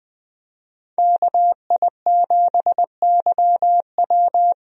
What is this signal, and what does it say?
Morse code, heard as a keyed tone, about 20 wpm, 700 Hz. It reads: KI7YW
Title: USA Call Signs - 5 Characters - Repeated 3x - 20wpm